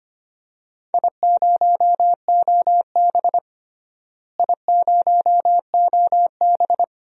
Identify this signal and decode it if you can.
Morse code, heard as a keyed tone, about 25 wpm, 700 Hz. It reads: I0OB I0OB
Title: DXCC-100 Call Signs - 4 Characters - Repeated 3x - 25wpm